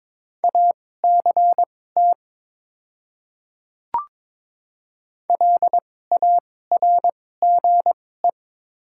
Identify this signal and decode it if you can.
Morse code, heard as a keyed tone, about 21 wpm, spaced as for 16 wpm, 700 Hz. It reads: ACT LARGE
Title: 2x Character Spacing - Top 200 Words - 22wpm